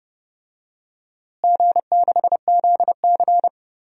Morse code, heard as a keyed tone, about 30 wpm, 700 Hz. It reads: G6ZC
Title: DXCC-100 Call Signs - 4 Characters - Repeated 2x - 30wpm